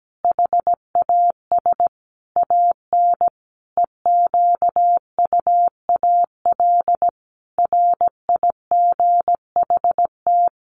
Morse code, heard as a keyed tone, about 17 wpm, 700 Hz. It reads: HAS AN EQUAL RIGHT
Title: Sentences from Top 1400 Words 17wpm